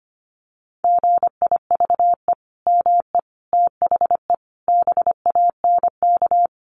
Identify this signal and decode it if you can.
Morse code, heard as a keyed tone, about 25 wpm, 700 Hz. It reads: GIVE ME THE BANK
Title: Sentences from Top 1200 Words 25wpm